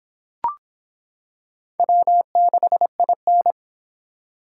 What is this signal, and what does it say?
Morse code, heard as a keyed tone, about 25 wpm, 700 Hz. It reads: W6IN